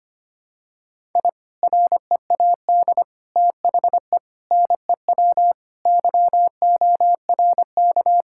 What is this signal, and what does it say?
Morse code, heard as a keyed tone, about 25 wpm, 700 Hz. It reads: I READ THE NEW YORK
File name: practice / 25wpm / Sentences with 3-Word Phrases - 25wpm